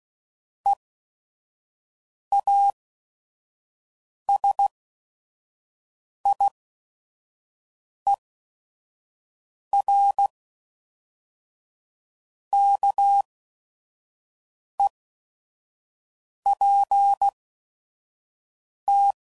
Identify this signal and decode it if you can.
Morse code, heard as a keyed tone, about 16 wpm, spaced as for 5 wpm, 800 Hz. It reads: EASIER KEPT